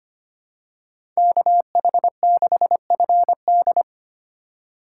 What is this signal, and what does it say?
Morse code, heard as a keyed tone, about 25 wpm, 700 Hz. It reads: KH6FD
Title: USA Call Signs - 5 Characters - 2x Character Spacing - 25wpm